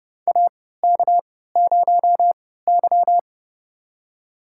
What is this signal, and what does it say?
Morse code, heard as a keyed tone, about 29 wpm, spaced as for 17 wpm, 700 Hz. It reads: AK0Y
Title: USA Call Signs - 4 Characters - 3x Character Spacing - 30wpm